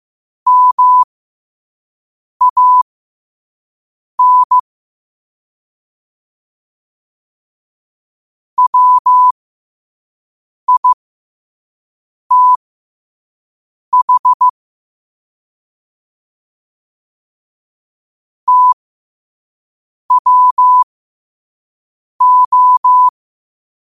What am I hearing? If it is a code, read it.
Morse code, heard as a keyed tone, about 15 wpm, spaced as for 5 wpm, 1000 Hz. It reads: MAN WITH TWO
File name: qotdfiles / 5wpm